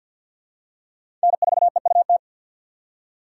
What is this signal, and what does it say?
Morse code, heard as a keyed tone, about 53 wpm, 700 Hz. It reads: N4EUT
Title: USA Call Signs - 5 Characters - 3x Character Spacing - 50wpm